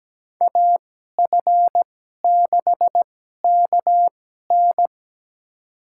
Morse code, heard as a keyed tone, about 17 wpm, spaced as for 12 wpm, 700 Hz. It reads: AF6KN
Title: USA Call Signs - 5 Characters - 2x Character Spacing - 17wpm